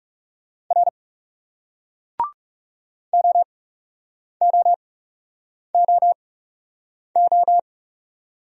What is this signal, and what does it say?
Morse code, heard as a keyed tone, about 38 wpm, spaced as for 8 wpm, 700 Hz. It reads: R OOOO